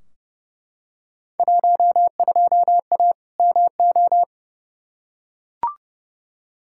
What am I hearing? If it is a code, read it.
Morse code, heard as a keyed tone, about 30 wpm, 700 Hz. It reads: 12A MO
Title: Field Day Exchanges Encore - 30wpm